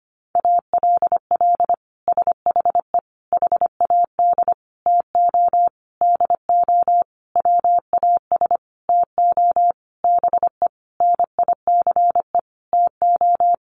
Morse code, heard as a keyed tone, about 25 wpm, 700 Hz. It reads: ALL SHE HAD TO DO WAS TO BE NICE TO
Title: Sentences from Top 1100 Words 25wpm